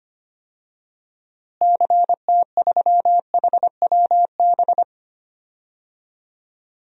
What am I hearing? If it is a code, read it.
Morse code, heard as a keyed tone, about 25 wpm, 700 Hz. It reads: CT3HWB
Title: DXCC-100 Call Signs - 6 Characters - Repeated 3x - 25wpm